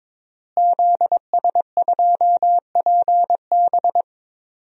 Morse code, heard as a keyed tone, about 22 wpm, 700 Hz. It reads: ZS2PB
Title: DXCC-100 Call Signs - 5 Characters - Repeated 2x - 22wpm